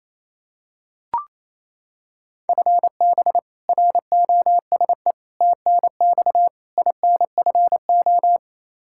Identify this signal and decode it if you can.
Morse code, heard as a keyed tone, about 28 wpm, 700 Hz. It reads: FB ROSE TNX INFO
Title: Practice QSO - Exchange 3 - More Info - 28wpm